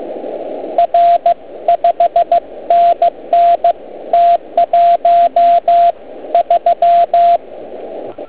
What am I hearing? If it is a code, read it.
Morse code, heard as a keyed tone, about 15 wpm, 690 Hz, with radio noise. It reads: R 5NN T13